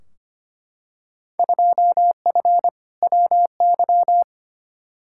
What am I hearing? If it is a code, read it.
Morse code, heard as a keyed tone, about 25 wpm, 700 Hz. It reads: 2F WY
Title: Field Day Exchanges Encore - 25wpm